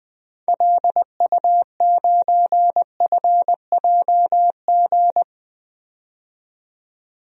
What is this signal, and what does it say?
Morse code, heard as a keyed tone, about 20 wpm, 700 Hz. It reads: LU9FJG
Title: DXCC-100 Call Signs - 6 Characters - 20wpm